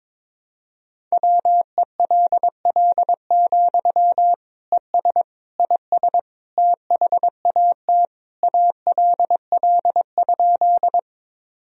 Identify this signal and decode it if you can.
Morse code, heard as a keyed tone, about 22 wpm, 700 Hz. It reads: WELL, ES IS THAT ALL?
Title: Sentences from Top 200 Words and Abbreviations 22wpm